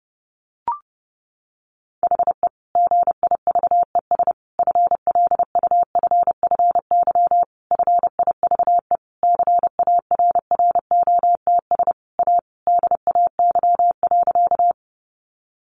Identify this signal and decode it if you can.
Morse code, heard as a keyed tone, about 30 wpm, 700 Hz. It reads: HE GIVES FLUFFY FIVE CARROTS A DAY.
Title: Easy Phrases 30wpm